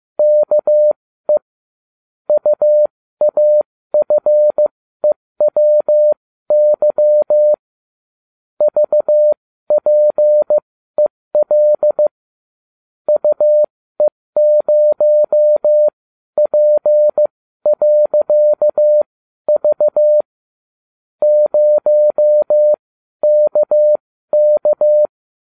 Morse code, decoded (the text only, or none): KE UAFEWY VPEL UE0P.V 0KK